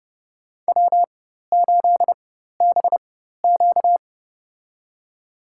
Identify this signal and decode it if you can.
Morse code, heard as a keyed tone, about 32 wpm, spaced as for 14 wpm, 700 Hz. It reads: W8BQ
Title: USA Call Signs - 4x Character Spacing - 30wpm